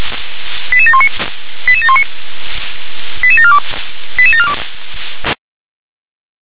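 Five-tone selective call (ZVEI), heard as quick runs of five tones, tones of about 70 ms. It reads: 88719 88719 88742 88742
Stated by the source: Five-tone ZVEI selective call received by SDR.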